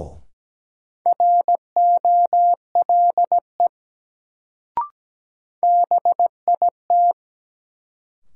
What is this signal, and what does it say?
Morse code, heard as a keyed tone, about 17 wpm, 700 Hz. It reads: ROLE BIT
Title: Sets of 3 Words - New Word Drill - 17wpm